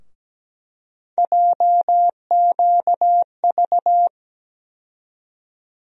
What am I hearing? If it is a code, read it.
Morse code, heard as a keyed tone, about 17 wpm, 700 Hz. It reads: JQV
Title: USA Call Sign Postfixes - 3 Letters 17wpm